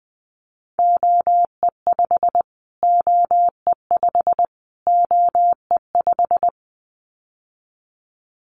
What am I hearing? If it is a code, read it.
Morse code, heard as a keyed tone, about 19 wpm, 700 Hz. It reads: OE5 OE5 OE5
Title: Callsign Prefixes for Top 25 DXCC - Repeated 3x - 20wpm